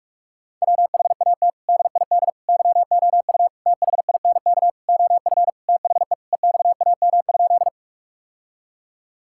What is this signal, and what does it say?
Morse code, heard as a keyed tone, about 42 wpm, 700 Hz. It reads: WHAT DID YOU THINK OF THE EXAM?
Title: Sentences from Top 1000 Words - Review Part 5 - 45wpm